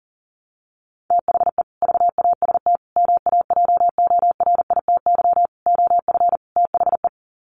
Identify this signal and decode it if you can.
Morse code, heard as a keyed tone, about 40 wpm, 700 Hz. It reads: THE VAST MAJORITY OF THE
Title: Sets of 5 Words Encore 40wpm